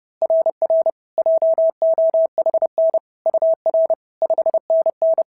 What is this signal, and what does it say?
Morse code, heard as a keyed tone, about 30 wpm, 690 Hz. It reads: RR JOHN UR 5NN
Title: SOTA QSO Practice 30wpm